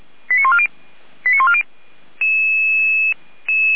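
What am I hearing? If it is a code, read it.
Five-tone selective call (ZVEI), heard as quick runs of five tones, tones of about 70 ms. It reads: 89140 89140
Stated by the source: ZVEI five-tone call received by SDR.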